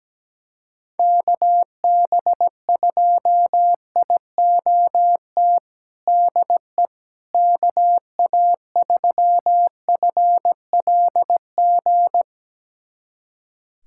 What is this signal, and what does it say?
Morse code, heard as a keyed tone, about 17 wpm, 700 Hz. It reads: KB2IOT DE KA3FLG
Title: USA Callsign DE Callsign - 6 Characters - 17wpm